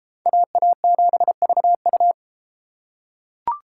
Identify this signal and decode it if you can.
Morse code, heard as a keyed tone, about 31 wpm, 700 Hz. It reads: AA7VU